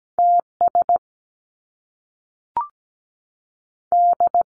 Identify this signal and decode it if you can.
Morse code, heard as a keyed tone, about 17 wpm, 700 Hz. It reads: TS D
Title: Top Two Letter Combinations in Words 17wpm